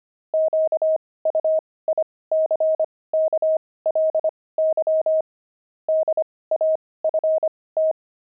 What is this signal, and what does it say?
Morse code, heard as a keyed tone, about 25 wpm, spaced as for 18 wpm, 630 Hz. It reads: QUICKLY DAFT
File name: pangram-fs-18-cs-25-daft